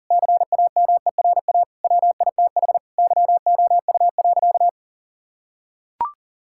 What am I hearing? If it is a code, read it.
Morse code, heard as a keyed tone, about 40 wpm, 700 Hz. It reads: CAMERA WITH YOU.
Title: Sentences from Top 1000 Words - Review Part 5 - 40wpm